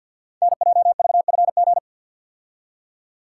Morse code, heard as a keyed tone, about 46 wpm, 700 Hz. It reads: NJVFG
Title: Code Groups - 5 Letters - 50wpm